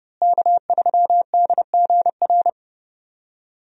Morse code, heard as a keyed tone, about 30 wpm, 700 Hz. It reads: K3DGR
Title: USA Call Signs - 5 Characters - Repeated 3x - 30wpm